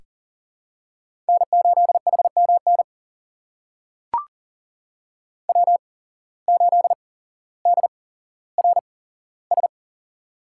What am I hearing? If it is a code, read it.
Morse code, heard as a keyed tone, about 37 wpm, spaced as for 17 wpm, 700 Hz. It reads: N8HMN W8DRS
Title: USA Call Signs - 5 Characters - 8x Character Spacing - 40wpm